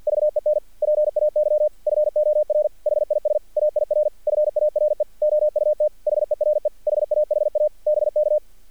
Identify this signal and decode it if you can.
Morse code, heard as a keyed tone, about 48 wpm, 600 Hz. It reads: VEN PAY FOR HIS AIR FARE OUT HERE HAHA BK
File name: archived / CRSnet